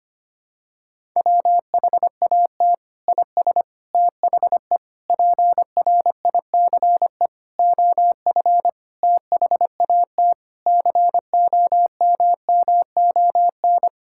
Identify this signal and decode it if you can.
Morse code, heard as a keyed tone, about 25 wpm, 700 Hz. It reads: WHAT IS THE PRICE OF THAT COMMON
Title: Sentences from Top 1000 Words - Review Part 5 - 25wpm